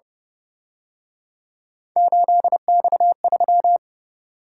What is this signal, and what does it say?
Morse code, heard as a keyed tone, about 29 wpm, 700 Hz. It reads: 8X3
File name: Speed-Racing - Three Letter-Number Combinations - 25wpm